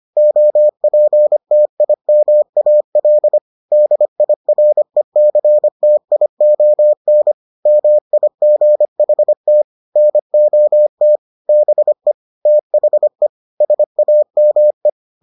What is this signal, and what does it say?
Morse code, heard as a keyed tone, about 25 wpm, 600 Hz. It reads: OPTIMAL DIRECTION MIGHT NOT BE THE SAME